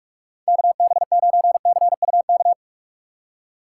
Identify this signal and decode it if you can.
Morse code, heard as a keyed tone, about 46 wpm, 700 Hz. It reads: KB9CUK
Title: USA Call Signs - 6 Characters - 2x Character Spacing - 45wpm